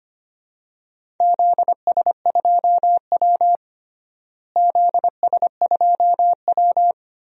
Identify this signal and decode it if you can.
Morse code, heard as a keyed tone, about 25 wpm, 700 Hz. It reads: ZS2W ZS2W
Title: DXCC-100 Call Signs - 4 Characters - Repeated 3x - 25wpm